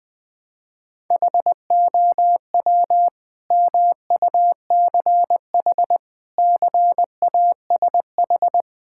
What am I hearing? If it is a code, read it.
Morse code, heard as a keyed tone, about 20 wpm, 700 Hz. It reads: HOW MUCH CASH
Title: Sentences from Top 1000 Words - Review Part 5 - 20wpm